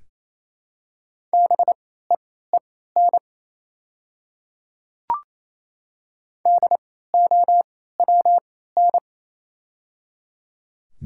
Morse code, heard as a keyed tone, about 28 wpm, spaced as for 16 wpm, 700 Hz. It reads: BEEN DOWN
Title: 3x Character Spacing - Top 200 Words - 28wpm